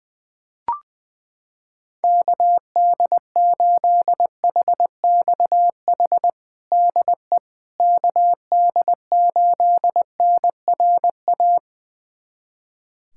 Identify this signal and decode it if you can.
Morse code, heard as a keyed tone, about 20 wpm, 700 Hz. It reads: KD8HXH DE KD8NRA
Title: USA Callsign DE Callsign 20wpm